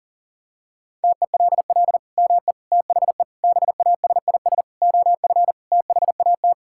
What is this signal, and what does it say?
Morse code, heard as a keyed tone, about 37 wpm, 700 Hz. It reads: TELL ME THE BASIS OF THAT
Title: Sentences from Top 1000 Words - Review Part 5 - 40wpm